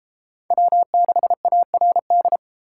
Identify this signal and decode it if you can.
Morse code, heard as a keyed tone, about 32 wpm, 700 Hz. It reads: W6ARD